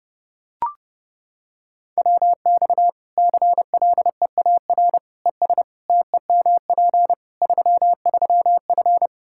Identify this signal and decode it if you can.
Morse code, heard as a keyed tone, about 32 wpm, 700 Hz. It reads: WX CLEAR ES TEMP 33F
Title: Practice QSO - Exchange 2 - Info - 30wpm